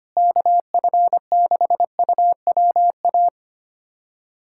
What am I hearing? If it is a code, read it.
Morse code, heard as a keyed tone, about 25 wpm, 700 Hz. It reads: KF6UWA
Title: USA Callsign DE Callsign 25wpm